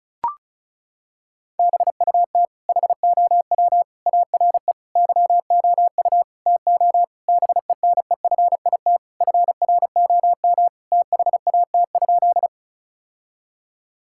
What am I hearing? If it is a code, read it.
Morse code, heard as a keyed tone, about 36 wpm, 700 Hz. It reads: BUT HOW ARE YOU TO BENEFIT FROM THAT?
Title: Sentences from Top 1000 Words - Review Part 6 - 35wpm